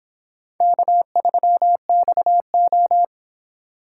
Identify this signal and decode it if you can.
Morse code, heard as a keyed tone, about 26 wpm, 700 Hz. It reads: K3XO